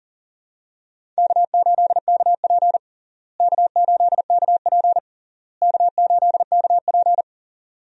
Morse code, heard as a keyed tone, about 37 wpm, 700 Hz. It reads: K8KP K8KP K8KP
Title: USA Call Signs - 4 Characters - Repeated 3x - 40wpm